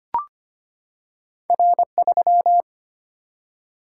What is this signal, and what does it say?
Morse code, heard as a keyed tone, about 25 wpm, 700 Hz. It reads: R3
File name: Two Letter-Number Combinations - 25wpm